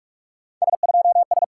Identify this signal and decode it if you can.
Morse code, heard as a keyed tone, about 44 wpm, 700 Hz. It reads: S2S
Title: SOTA QSO Practice 45wpm